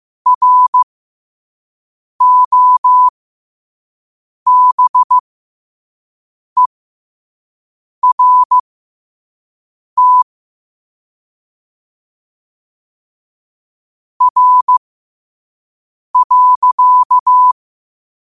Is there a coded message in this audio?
Yes: Morse code from 0.3 to 17.5 s, heard as a keyed tone, about 15 wpm, spaced as for 5 wpm, 1000 Hz. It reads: ROBERT R.